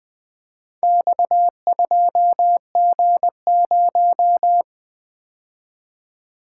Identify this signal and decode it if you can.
Morse code, heard as a keyed tone, about 20 wpm, 700 Hz. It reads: X2G0
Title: Four Letter-Number Combinations - 20wpm